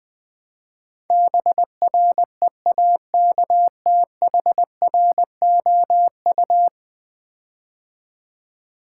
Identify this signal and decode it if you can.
Morse code, heard as a keyed tone, about 20 wpm, 700 Hz. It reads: BREAKTHROU
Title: Build A Word - 5 Letters At A Time - 12-Letter Words - 20wpm